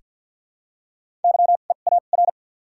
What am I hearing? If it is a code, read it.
Morse code, heard as a keyed tone, about 49 wpm, 700 Hz. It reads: Y E A R
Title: 2x Character Spacing - Top 300 Words - 50wpm